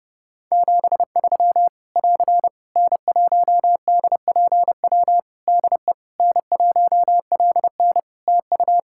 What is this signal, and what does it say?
Morse code, heard as a keyed tone, about 30 wpm, 700 Hz. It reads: 73 <AR> N1DPW DE N1LN TU
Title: Practice QSO - Exchange 4 - Ending - 30wpm